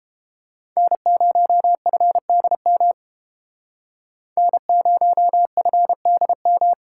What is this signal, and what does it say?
Morse code, heard as a keyed tone, about 31 wpm, 700 Hz. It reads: N0FDM N0FDM